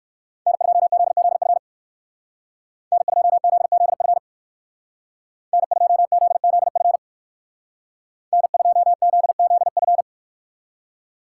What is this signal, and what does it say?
Morse code, heard as a keyed tone, about 54 wpm, 700 Hz. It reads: N2ZZF N2ZZF N2ZZF N2ZZF
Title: Speed Racing - USA 1x3 Call Signs 45wpm